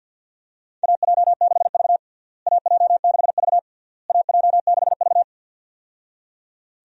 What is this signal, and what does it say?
Morse code, heard as a keyed tone, about 51 wpm, 700 Hz. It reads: AJ6V AJ6V AJ6V
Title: USA Call Signs - 4 Characters - Repeated 3x - 50wpm